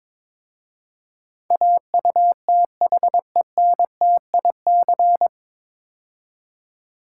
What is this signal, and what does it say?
Morse code, heard as a keyed tone, about 22 wpm, 700 Hz. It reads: AUTHENTIC
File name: 9 Letter Words - First 3 Letters Given - 22wpm